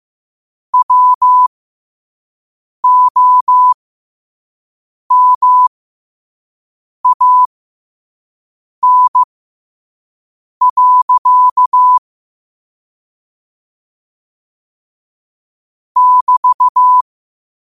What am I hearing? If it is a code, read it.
Morse code, heard as a keyed tone, about 15 wpm, spaced as for 5 wpm, 1000 Hz. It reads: WOMAN. <BT>
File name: qotdfiles / 5wpm